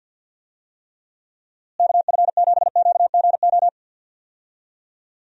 Morse code, heard as a keyed tone, about 48 wpm, 700 Hz. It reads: KF7QGO
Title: USA Call Signs - 6 Characters - Repeated 2x - 50wpm